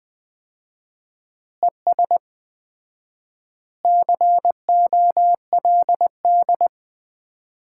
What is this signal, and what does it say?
Morse code, heard as a keyed tone, about 21 wpm, 700 Hz. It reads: ES COLD
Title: Common QSO Phrases - 4x Word-Spacing - 20wpm